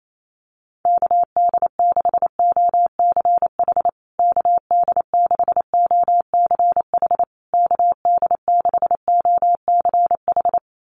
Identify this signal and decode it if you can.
Morse code, heard as a keyed tone, about 28 wpm, 690 Hz. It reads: KD6OCH KD6OCH KD6OCH
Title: USA Call Signs Repeated 3x 28wpm